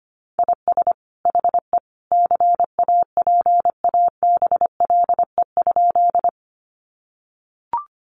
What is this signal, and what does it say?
Morse code, heard as a keyed tone, about 25 wpm, 700 Hz. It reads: IS HE CAPABLE?
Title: Sentences from Top 1200 Words 25wpm